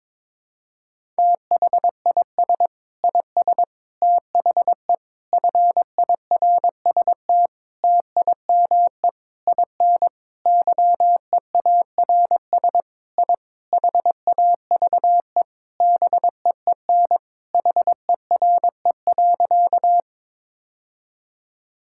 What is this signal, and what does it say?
Morse code, heard as a keyed tone, about 21 wpm, 700 Hz. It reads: THIS IS THE FIRST TIME IN YEARS I HAVE BEEN HERE.
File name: practice / 22wpm / Sentences with 4-Word Phrases - 22wpm